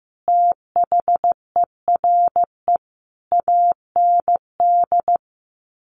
Morse code, heard as a keyed tone, about 15 wpm, 700 Hz. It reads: THERE AND
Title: Sets of 5 Words Encore2 15wpm